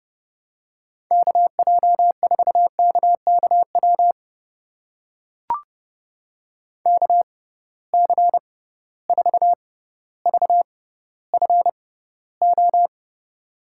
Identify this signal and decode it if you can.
Morse code, heard as a keyed tone, about 30 wpm, spaced as for 15 wpm, 700 Hz. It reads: KJ4KKW KC4VFO
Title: USA Call Signs - 6 Characters - 6x Character Spacing - 30wpm